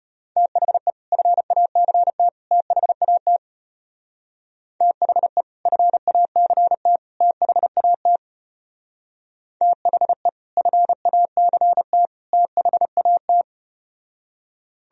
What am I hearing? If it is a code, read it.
Morse code, heard as a keyed tone, about 33 wpm, 700 Hz. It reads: THE FACT THAT THE FACT THAT THE FACT THAT